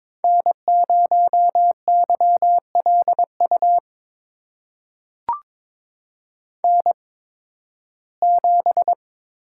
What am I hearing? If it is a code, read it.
Morse code, heard as a keyed tone, about 22 wpm, spaced as for 14 wpm, 700 Hz. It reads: N0YLU N7
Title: USA Call Signs - 5 Characters - 8x Character Spacing - 22wpm